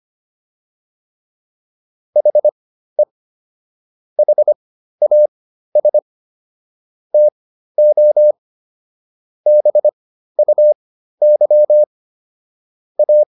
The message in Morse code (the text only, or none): HE HAS TO BUY A